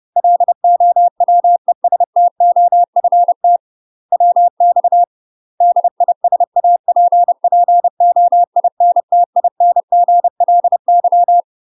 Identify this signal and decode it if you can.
Morse code, heard as a keyed tone, about 30 wpm, 700 Hz. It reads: LOWESTOFT WX DISAPPOINTINGLY
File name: UK MASTER QSOs TRAINING / 30WPM / UK MASTER QSOs 0050